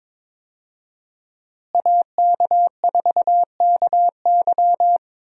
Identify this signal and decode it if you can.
Morse code, heard as a keyed tone, about 22 wpm, 700 Hz. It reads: AK4KY
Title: USA Call Signs - 5 Characters - Repeated 3x - 22wpm